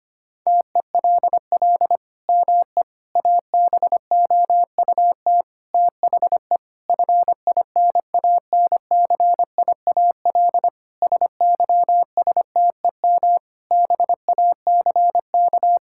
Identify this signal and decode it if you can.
Morse code, heard as a keyed tone, about 25 wpm, 700 Hz. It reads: TELL ME ABOUT THE FINANCIAL SYSTEM BACK
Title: Sentences from Top 1000 Words - Review Part 5 - 25wpm